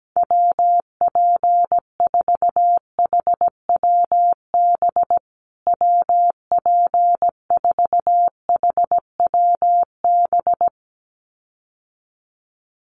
Morse code, heard as a keyed tone, about 17 wpm, 700 Hz. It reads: WP4HWB WP4HWB
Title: USA Call Signs Repeated 2x 17wpm